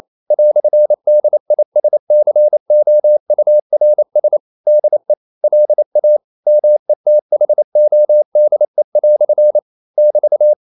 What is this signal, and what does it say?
Morse code, heard as a keyed tone, about 28 wpm, 600 Hz. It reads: "DISCOURS DE LA METHODE" <BT>
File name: CW / Practice / EnglishQuotations0128